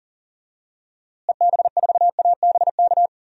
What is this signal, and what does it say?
Morse code, heard as a keyed tone, about 37 wpm, 700 Hz. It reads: EB4ABK
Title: DXCC-100 Call Signs - 6 Characters - Repeated 3x - 40wpm